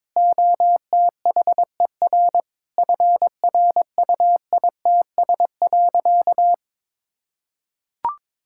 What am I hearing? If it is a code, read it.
Morse code, heard as a keyed tone, about 22 wpm, 700 Hz. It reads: OTHER FRUITS.
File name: Sentences with 7-Word Phrases - 22wpm